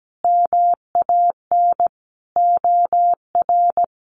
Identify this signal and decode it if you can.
Morse code, heard as a keyed tone, about 17 wpm, 700 Hz. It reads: MAN OR
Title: Sentences from Top 1400 Words 17wpm